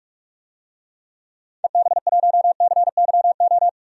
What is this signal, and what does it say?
Morse code, heard as a keyed tone, about 44 wpm, 690 Hz. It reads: EB1CYO
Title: DXCC-100 Call Signs - 6 Characters - 45wpm